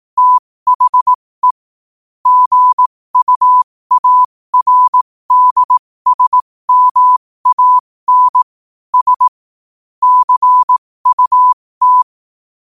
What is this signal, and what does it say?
Morse code, heard as a keyed tone, about 18 wpm, spaced as for 15 wpm, 1000 Hz. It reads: THE GUARDSMAN S CUT
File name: qotdfiles / 15wpm